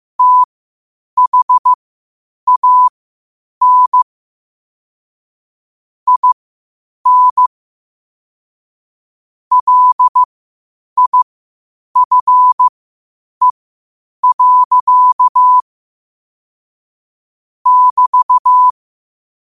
Morse code, heard as a keyed tone, about 15 wpm, spaced as for 8 wpm, 1000 Hz. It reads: THAN IN LIFE. <BT>
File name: qotdfiles / 8wpm